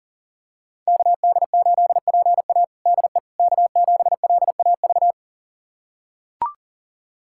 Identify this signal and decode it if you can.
Morse code, heard as a keyed tone, about 37 wpm, 700 Hz. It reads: KD8PA DE K7LAV
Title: USA Callsign DE Callsign - 5 Characters - 40wpm